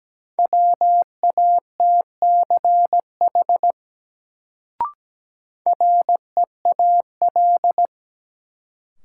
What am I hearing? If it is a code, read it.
Morse code, heard as a keyed tone, about 17 wpm, 700 Hz. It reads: WATCH REAL
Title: Sentences from Top 300 Words - New Word Drill - 17wpm